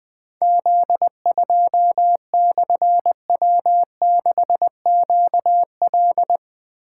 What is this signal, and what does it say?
Morse code, heard as a keyed tone, about 21 wpm, 700 Hz. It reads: Z2/W6QL
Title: Difficult Call Signs 20wpm